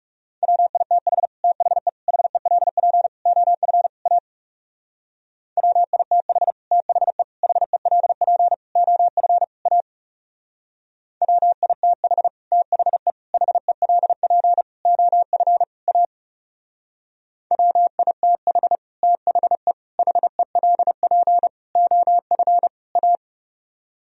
Morse code, heard as a keyed tone, about 37 wpm, 700 Hz. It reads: WITH THE HELP OF A WITH THE HELP OF A WITH THE HELP OF A WITH THE HELP OF A
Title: Speed-Racing - Sets of 5 Words 30wpm